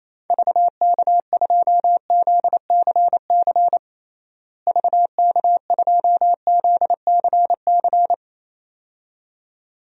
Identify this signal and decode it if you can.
Morse code, heard as a keyed tone, about 27 wpm, 700 Hz. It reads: VK2ZCC VK2ZCC
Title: DXCC-100 Call Signs - 6 Characters - Repeated 2x - 28wpm